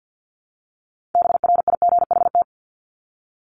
Morse code, heard as a keyed tone, about 50 wpm, 700 Hz. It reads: BRIGHT
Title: Top 751-1000 Words 50wpm